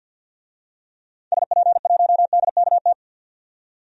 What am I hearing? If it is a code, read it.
Morse code, heard as a keyed tone, about 48 wpm, 700 Hz. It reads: SP1DKT
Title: DXCC-100 Call Signs - 6 Characters - 50wpm